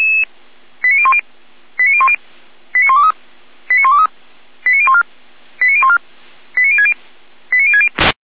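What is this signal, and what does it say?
Five-tone selective call (ZVEI), heard as quick runs of five tones, tones of about 70 ms. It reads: 89010 89010 89123 89123 89014 89014 89060 89060
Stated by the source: ZVEI five-tone selective call received by SDR.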